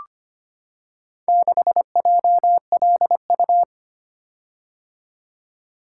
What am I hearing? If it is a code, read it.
Morse code, heard as a keyed tone, about 25 wpm, 690 Hz. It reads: 6JLU
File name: practice / 25wpm / Four Letter-Number Combinations - 25wpm